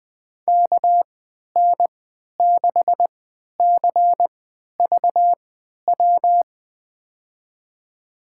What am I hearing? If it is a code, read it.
Morse code, heard as a keyed tone, about 20 wpm, spaced as for 11 wpm, 700 Hz. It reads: KN6CVW
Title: USA Call Signs - 6 Characters - 3x Character Spacing - 20wpm